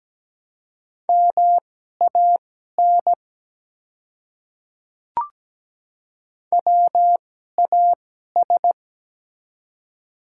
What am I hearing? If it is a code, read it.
Morse code, heard as a keyed tone, about 17 wpm, spaced as for 12 wpm, 700 Hz. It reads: MAN WAS
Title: 2x Character Spacing - Top 200 Words - 17wpm